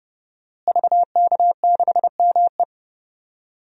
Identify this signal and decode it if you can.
Morse code, heard as a keyed tone, about 31 wpm, 700 Hz. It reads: VK6ME
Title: DXCC-100 Call Signs - 5 Characters - Repeated 2x - 30wpm